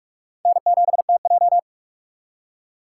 Morse code, heard as a keyed tone, about 46 wpm, 700 Hz. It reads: N7TJ
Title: USA Call Signs - 4 Characters - 3x Character Spacing - 45wpm